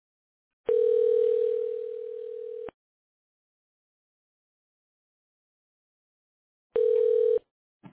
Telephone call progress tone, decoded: ringback tone